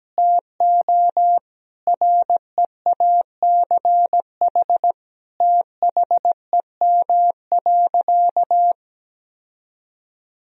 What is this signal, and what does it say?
Morse code, heard as a keyed tone, about 17 wpm, 700 Hz. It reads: TO REACH THEM.
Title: Sentences from Top 1000 Words - Review Part 2 - 17wpm